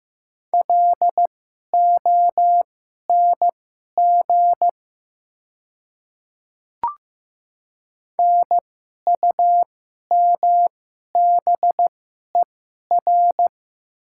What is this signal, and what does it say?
Morse code, heard as a keyed tone, about 15 wpm, spaced as for 11 wpm, 690 Hz. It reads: LONG NUMBER
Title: 2x Character Spacing - Top 100 Words - 15wpm